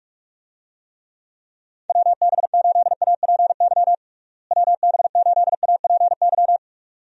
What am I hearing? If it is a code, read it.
Morse code, heard as a keyed tone, about 43 wpm, 700 Hz. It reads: WB8APY WB8APY
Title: USA Call Signs - 6 Characters - Repeated 2x - 45wpm